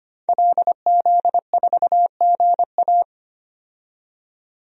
Morse code, heard as a keyed tone, about 25 wpm, 700 Hz. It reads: LZ4GA
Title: DXCC-100 Call Signs - 5 Characters - 25wpm